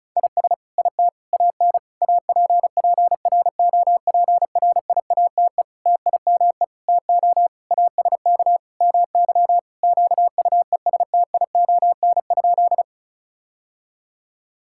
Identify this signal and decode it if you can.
Morse code, heard as a keyed tone, about 36 wpm, 700 Hz. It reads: IS IT AN APPROPRIATE TIME TO ASK MY QUESTION?
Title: Sentences from Top 1000 Words - Review Part 5 - 35wpm